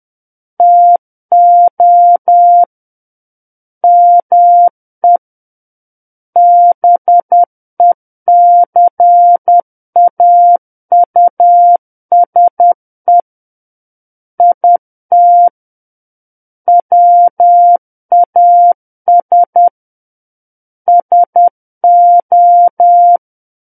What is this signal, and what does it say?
Morse code, heard as a keyed tone, about 10 wpm, 700 Hz. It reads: TO ME BECAUSE IT WAS SO
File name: Book One / 01-Chapter 1 - 10 WPM